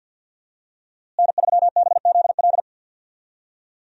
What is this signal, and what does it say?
Morse code, heard as a keyed tone, about 52 wpm, 700 Hz. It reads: N3BZL
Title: USA Call Signs - 5 Characters - Repeated 3x - 50wpm